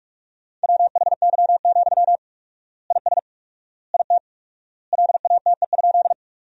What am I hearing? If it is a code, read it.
Morse code, heard as a keyed tone, about 43 wpm, 700 Hz. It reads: WHY, IS IT LATE?
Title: 4x Word-Spacing - Sentences from Top 300 Words - 45wpm